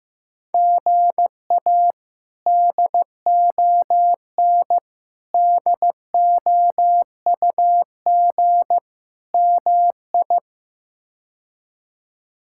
Morse code, heard as a keyed tone, about 15 wpm, 700 Hz. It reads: GA DON DOUG MI
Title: K1USN SST QSO Practice - 15wpm